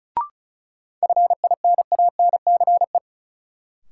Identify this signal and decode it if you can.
Morse code, heard as a keyed tone, about 36 wpm, 700 Hz. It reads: FINANCE
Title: Sentences from Top 1500 Words - New Word Drill - 35wpm